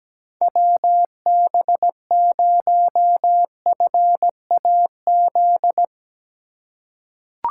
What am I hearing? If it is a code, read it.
Morse code, heard as a keyed tone, about 17 wpm, 700 Hz. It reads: WB0FAZ
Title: USA Call Signs - 2x Character Spacing - 17wpm